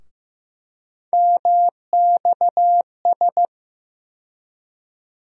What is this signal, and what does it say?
Morse code, heard as a keyed tone, about 15 wpm, 700 Hz. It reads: MXS
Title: USA Call Sign Postfixes - 3 Letters 15wpm